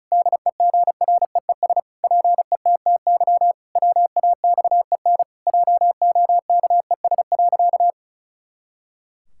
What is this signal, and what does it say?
Morse code, heard as a keyed tone, about 36 wpm, 700 Hz. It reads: DEGREES PETTY WAXEN JOKES.
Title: Pangram Challenge v2 - 35wpm